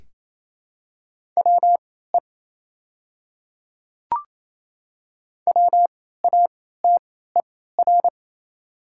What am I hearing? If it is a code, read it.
Morse code, heard as a keyed tone, about 28 wpm, spaced as for 16 wpm, 700 Hz. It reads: WE WATER
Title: 3x Character Spacing - Top 100 Words - 28wpm